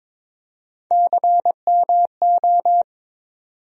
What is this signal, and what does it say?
Morse code, heard as a keyed tone, about 21 wpm, 700 Hz. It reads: CMO
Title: USA Call Sign Postfixes - 3 Letters 22wpm